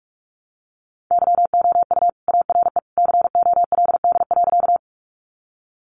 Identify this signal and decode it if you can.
Morse code, heard as a keyed tone, about 44 wpm, 700 Hz. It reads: YOU ARE COLD.
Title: Sentences from Top 400 Words 45wpm